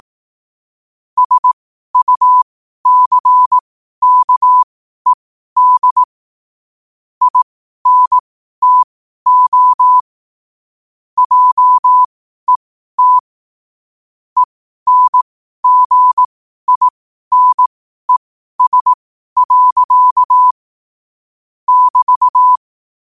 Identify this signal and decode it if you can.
Morse code, heard as a keyed tone, about 18 wpm, spaced as for 12 wpm, 1000 Hz. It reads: SUCKED INTO JET ENGINES. <BT>
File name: qotdfiles / 12wpm